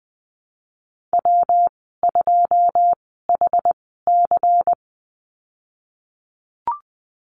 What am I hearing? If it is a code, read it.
Morse code, heard as a keyed tone, about 20 wpm, spaced as for 15 wpm, 700 Hz. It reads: W2HC
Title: 154 Lesson 25g - Letter p - Call Signs 20wpm